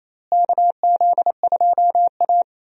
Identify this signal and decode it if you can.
Morse code, heard as a keyed tone, about 28 wpm, 700 Hz. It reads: KZ2A
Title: USA Call Signs - 4 Characters - Repeated 3x - 28wpm